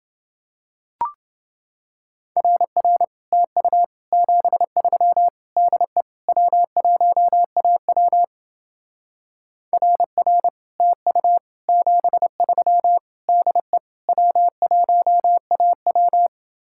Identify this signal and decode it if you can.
Morse code, heard as a keyed tone, about 30 wpm, 700 Hz. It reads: RR TU 73 DE W1AW RR TU 73 DE W1AW